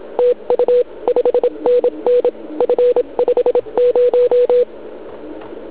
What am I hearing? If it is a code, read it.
Morse code, heard as a keyed tone, about 27 wpm, 490 Hz. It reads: TU 5NN F50
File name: ARCHIV / zvuk2 / CW RUN